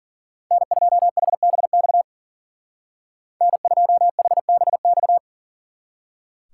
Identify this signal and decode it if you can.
Morse code, heard as a keyed tone, about 52 wpm, 700 Hz. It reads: N2HBX N2HBX